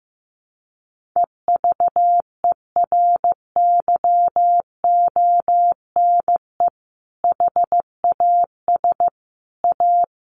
Morse code, heard as a keyed tone, about 15 wpm, 700 Hz. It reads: EVERYONE HAS A
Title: Sentences from Top 1200 Words 15wpm